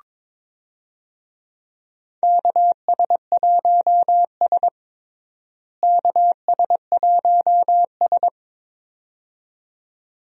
Morse code, heard as a keyed tone, about 22 wpm, 700 Hz. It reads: KS1S KS1S
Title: USA Call Signs - 4 Characters - Repeated 2x - 22wpm